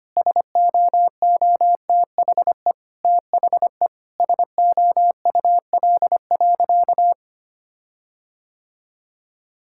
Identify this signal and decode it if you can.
Morse code, heard as a keyed tone, about 25 wpm, 700 Hz. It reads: SOOTHE THE SOUL.
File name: practice / 25wpm / Difficult Sentences with Hints - 25wpm